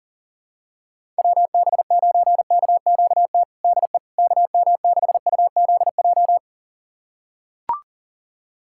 Morse code, heard as a keyed tone, about 40 wpm, 700 Hz. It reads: WB9KQT DE KM6UZJ
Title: USA Callsign DE Callsign - 6 Characters - 40wpm